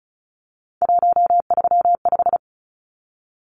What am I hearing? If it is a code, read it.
Morse code, heard as a keyed tone, about 34 wpm, 700 Hz. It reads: 135
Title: Numbers - 3-Digit 35wpm